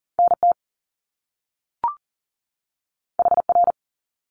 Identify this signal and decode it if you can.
Morse code, heard as a keyed tone, about 42 wpm, 700 Hz. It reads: NT HR
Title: Top Two Letter Combinations in Words 40wpm